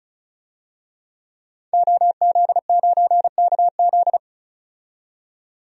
Morse code, heard as a keyed tone, about 35 wpm, 700 Hz. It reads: OZ9KZ
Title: DXCC-100 Call Signs - 5 Characters - 35wpm